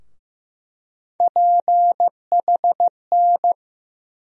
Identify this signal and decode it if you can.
Morse code, heard as a keyed tone, about 15 wpm, 700 Hz. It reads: PHN
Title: USA Call Sign Postfixes - 3 Letters 15wpm